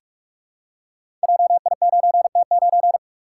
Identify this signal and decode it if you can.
Morse code, heard as a keyed tone, about 44 wpm, 700 Hz. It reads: JI9T9
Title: Code Groups - 5 Letter:Numbers - 45wpm